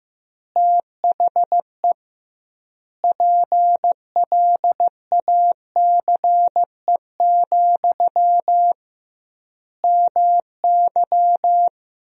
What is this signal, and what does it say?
Morse code, heard as a keyed tone, about 15 wpm, 700 Hz. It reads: THE PLACE, MY
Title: 2x Word-Spacing - Sentences from Top 400 Words - 15wpm